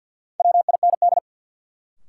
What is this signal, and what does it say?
Morse code, heard as a keyed tone, about 53 wpm, 700 Hz. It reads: WIND
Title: Sentences from Top 400 Words - New Word Drill - 50wpm